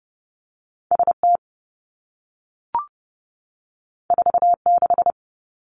Morse code, heard as a keyed tone, about 30 wpm, 700 Hz. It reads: S T 4 6
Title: Two Challenging Letter-Number Combinations 30wpm